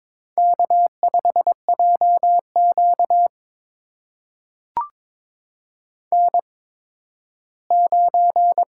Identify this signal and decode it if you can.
Morse code, heard as a keyed tone, about 22 wpm, spaced as for 13 wpm, 700 Hz. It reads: K5JQ N9
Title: USA Call Signs - 4 Characters - 8x Character Spacing - 22wpm